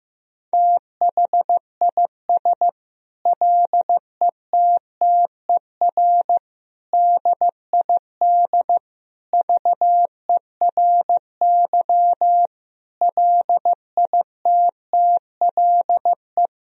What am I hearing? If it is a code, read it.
Morse code, heard as a keyed tone, about 16 wpm, 700 Hz. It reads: THIS LETTER DID VERY LITTLE
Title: Sentences from Top 1000 Words - Review Part 4 - 15wpm